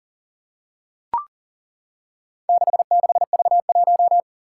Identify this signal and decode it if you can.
Morse code, heard as a keyed tone, about 40 wpm, 700 Hz. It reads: 66V1
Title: Four Challenging Letter-Number Combinations 40wpm